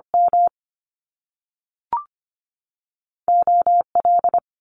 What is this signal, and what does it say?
Morse code, heard as a keyed tone, about 25 wpm, 700 Hz. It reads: M O L
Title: Top Two Letter Combinations in Words 25wpm